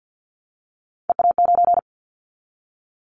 Morse code, heard as a keyed tone, about 50 wpm, 690 Hz. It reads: EA9
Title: Callsign Prefixes for Top 100 DXCC - 50wpm